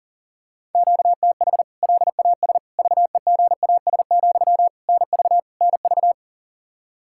Morse code, heard as a keyed tone, about 40 wpm, 700 Hz. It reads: QTH LAS VEGAS, NV NV
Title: K1USN SST QSO Practice 40wpm